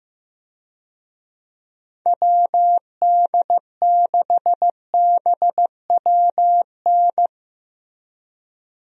Morse code, heard as a keyed tone, about 15 wpm, 700 Hz. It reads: WD6BWN
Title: USA Call Signs - 6 Characters - Repeated 2x - 15wpm